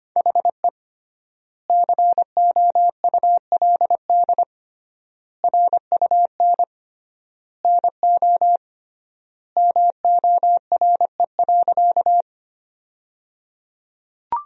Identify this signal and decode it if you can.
Morse code, heard as a keyed tone, about 25 wpm, 700 Hz. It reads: HE COULD RUN NO MORE.
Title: 3x Word-Spacing - Sentences from Top 300 Words - 25wpm